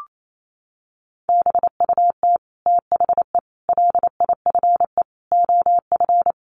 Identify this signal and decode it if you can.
Morse code, heard as a keyed tone, about 28 wpm, 700 Hz. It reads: BUT THE LIFE OF
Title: Sentences from Top 1500 Words 28wpm